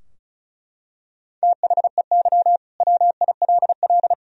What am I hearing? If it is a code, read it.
Morse code, heard as a keyed tone, about 36 wpm, 700 Hz. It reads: THEY WILL
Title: Sentences from Top 1000 Words - Review Part 4 - 35wpm